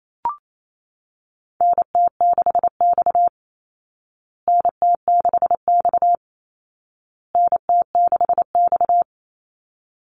Morse code, heard as a keyed tone, about 28 wpm, 700 Hz. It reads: NT6X NT6X NT6X
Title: Call Signs Repeated 2x - 28wpm